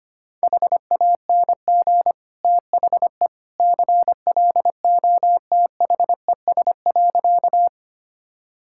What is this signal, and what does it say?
Morse code, heard as a keyed tone, about 25 wpm, 700 Hz. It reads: HANG THE CLOTHES.
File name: Easy Sentences with Hints - 25wpm